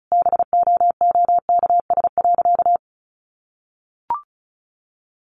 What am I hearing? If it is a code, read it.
Morse code, heard as a keyed tone, about 34 wpm, 700 Hz. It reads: BOOKS.
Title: Sentences from Top 1200 Words 35wpm